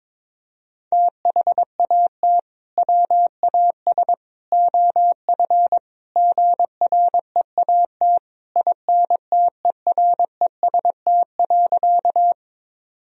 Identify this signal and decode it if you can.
Morse code, heard as a keyed tone, about 22 wpm, 700 Hz. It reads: THAT WAS OF GREAT INTEREST.
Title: Sentences from Top 1000 Words - Review Part 2 - 22wpm